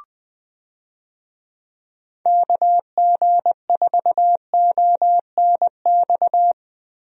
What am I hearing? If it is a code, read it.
Morse code, heard as a keyed tone, about 20 wpm, 700 Hz. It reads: KG4ONX
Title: USA Call Signs - 6 Characters - Repeated 2x - 20wpm